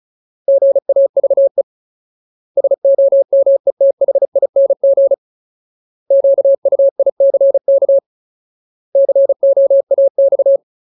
Morse code, heard as a keyed tone, about 34 wpm, 550 Hz. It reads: GAVE SOMETHING QUICK COAX
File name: Ultimate Word List / 35WPM Ultimate